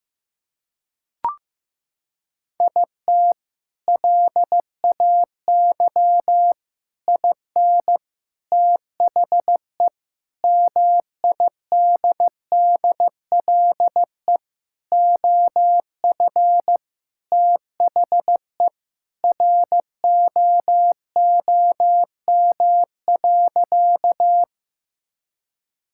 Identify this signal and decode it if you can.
Morse code, heard as a keyed tone, about 16 wpm, 700 Hz. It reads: IT LAY IN THE MIDDLE OF THE ROOM.
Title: Sentences from Top 1000 Words - Review Part 6 - 15wpm